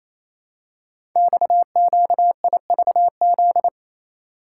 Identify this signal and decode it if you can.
Morse code, heard as a keyed tone, about 28 wpm, 700 Hz. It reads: XQIVZ
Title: Code Groups - 5 Letters - 28wpm